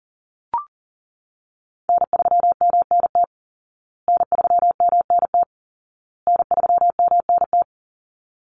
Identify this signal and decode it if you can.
Morse code, heard as a keyed tone, about 40 wpm, 700 Hz. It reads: N3MNT N3MNT N3MNT
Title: Call Signs Repeated 2x - 40wpm